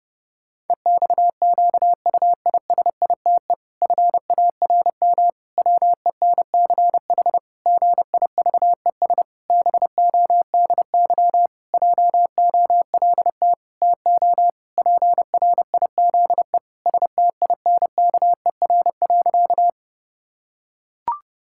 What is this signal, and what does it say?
Morse code, heard as a keyed tone, about 30 wpm, 700 Hz. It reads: EXQUISITE FARM WENCH GIVES BODY JOLT TO PRIZE STINKER.
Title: Pangram Challenge v2 - 30wpm